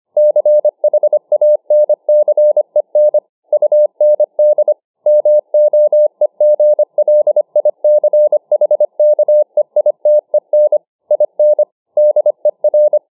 Morse code, heard as a keyed tone, about 25 wpm, 600 Hz. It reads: CHANCEN UND MOEGLICHKEITEN IN DER